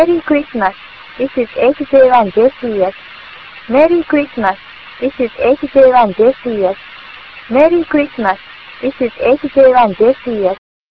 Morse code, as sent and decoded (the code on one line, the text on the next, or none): none